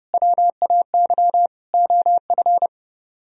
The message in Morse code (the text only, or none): WAY OF